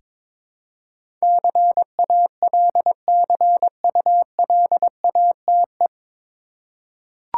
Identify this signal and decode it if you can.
Morse code, heard as a keyed tone, about 22 wpm, 700 Hz. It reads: CALCULATE
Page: Morse Code Ninja practice 22wpm